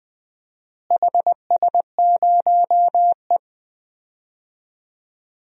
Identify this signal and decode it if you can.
Morse code, heard as a keyed tone, about 20 wpm, 700 Hz. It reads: HS0E
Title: DXCC-100 Call Signs - 4 Characters - Repeated 3x - 20wpm